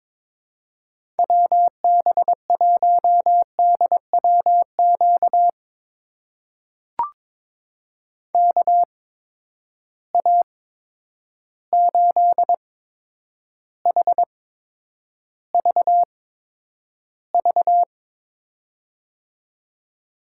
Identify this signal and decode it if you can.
Morse code, heard as a keyed tone, about 22 wpm, spaced as for 9 wpm, 700 Hz. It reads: WB1DWQ KA8HVV
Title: USA Call Signs - 6 Characters - 8x Character Spacing - 22wpm